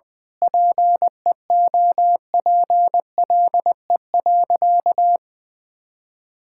Morse code, heard as a keyed tone, about 20 wpm, 700 Hz. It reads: PEOPLE.
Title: Sentences with 9-Word Phrases - Spoken Beginning - 20wpm